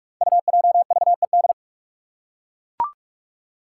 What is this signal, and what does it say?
Morse code, heard as a keyed tone, about 44 wpm, 700 Hz. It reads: UJVED
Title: Five Challenging Letter-Number Combinations 45wpm